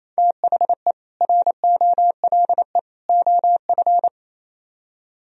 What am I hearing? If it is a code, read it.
Morse code, heard as a keyed tone, about 35 wpm, 700 Hz. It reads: THE ROLE OF
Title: Speed-Racing - Sets of 3 Words 28wpm